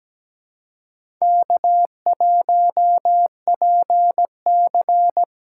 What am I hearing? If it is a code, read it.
Morse code, heard as a keyed tone, about 17 wpm, 700 Hz. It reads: K1PC
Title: USA Call Signs - 4 Characters - 3x Character Spacing - 17wpm